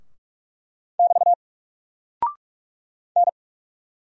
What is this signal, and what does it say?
Morse code, heard as a keyed tone, about 44 wpm, 700 Hz. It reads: <BT> N
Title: Single Character (Letter:Number:Punctuation:Prosign) 45wpm